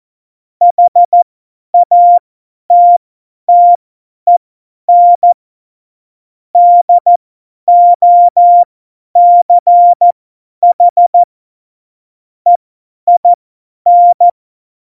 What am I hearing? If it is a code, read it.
Morse code, heard as a keyed tone, about 14 wpm, spaced as for 10 wpm, 700 Hz. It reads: HATTEN DOCH EIN
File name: Morsekurs / Witze / Witz 02. Juli 2019, 50 F70